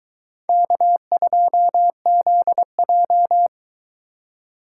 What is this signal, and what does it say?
Morse code, heard as a keyed tone, about 22 wpm, 700 Hz. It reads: K2ZJ